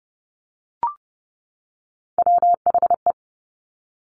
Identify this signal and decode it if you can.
Morse code, heard as a keyed tone, about 30 wpm, 700 Hz. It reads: WHE
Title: Top Three Letter Combinations in Words 30wpm